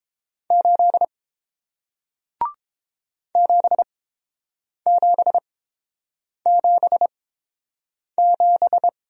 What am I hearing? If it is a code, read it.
Morse code, heard as a keyed tone, about 29 wpm, spaced as for 7 wpm, 700 Hz. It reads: 8 7777